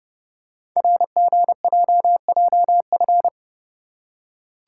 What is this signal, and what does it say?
Morse code, heard as a keyed tone, about 29 wpm, 690 Hz. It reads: RGJJF
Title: Code Groups - 5 Letters - 30wpm